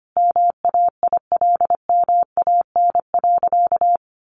Morse code, heard as a keyed tone, about 25 wpm, 700 Hz. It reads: MAILMAN.
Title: Additional Easy Phrases 25wpm